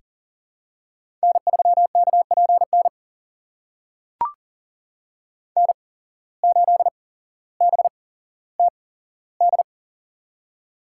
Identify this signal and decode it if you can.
Morse code, heard as a keyed tone, about 40 wpm, spaced as for 17 wpm, 700 Hz. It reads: N3KPN N8BTD
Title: USA Call Signs - 5 Characters - 8x Character Spacing - 40wpm